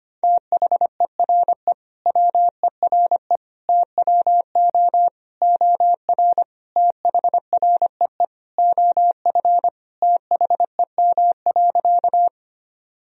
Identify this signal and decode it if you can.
Morse code, heard as a keyed tone, about 25 wpm, 700 Hz. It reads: THERE WERE TWO OR THREE OF THEM.
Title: Sentences from Top 1000 Words - Review Part 1 - 25wpm